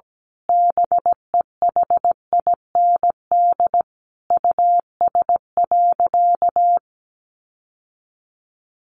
Morse code, heard as a keyed tone, about 17 wpm, 700 Hz. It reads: BEHIND US.
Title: Sentences from Top 1300 Words 17wpm